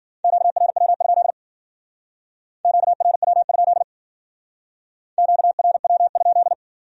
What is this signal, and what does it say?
Morse code, heard as a keyed tone, about 49 wpm, 700 Hz. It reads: QRW? QRW? QRW?
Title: Speed-Racing Q-Codes - 40wpm